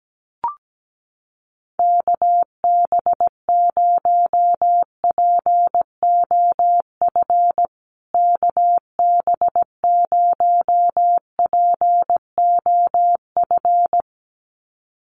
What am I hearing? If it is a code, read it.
Morse code, heard as a keyed tone, about 17 wpm, 700 Hz. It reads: KB0POF KB0POF
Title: USA Call Signs Repeated 2x 17wpm